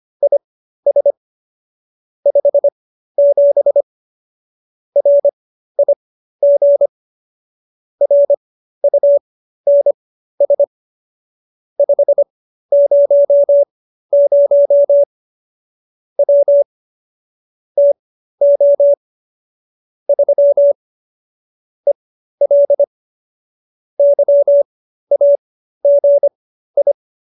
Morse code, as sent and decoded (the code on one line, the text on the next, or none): .. ... / ..... --... / .-. .. --. / .-. ..- -. ... / ..... ----- ----- / .-- / - --- / ...-- / . .-.. / -.-- .- --. ..
IS 57 RIG RUNS 500 W TO 3 EL YAGI